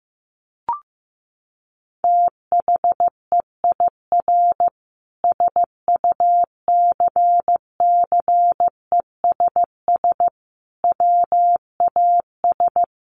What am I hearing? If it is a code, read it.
Morse code, heard as a keyed tone, about 15 wpm, 700 Hz. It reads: THEIR SUCCESS WAS
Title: Sentences from Top 1500 Words 15wpm